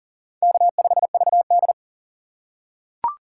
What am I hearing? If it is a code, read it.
Morse code, heard as a keyed tone, about 40 wpm, 700 Hz. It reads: K5VD
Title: USA Call Signs - 2x Character Spacing - 40wpm